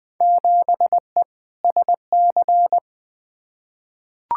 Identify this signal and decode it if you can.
Morse code, heard as a keyed tone, about 21 wpm, 700 Hz. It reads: 7E SC
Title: Field Day Exchanges Encore - 20wpm